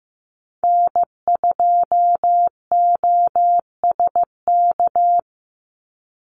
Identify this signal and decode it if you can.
Morse code, heard as a keyed tone, about 15 wpm, 700 Hz. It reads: N2OSK
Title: USA Call Signs 15wpm